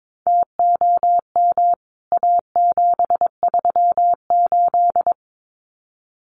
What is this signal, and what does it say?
Morse code, heard as a keyed tone, about 22 wpm, 700 Hz. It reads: TOM A738
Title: CWT Exchange 22wpm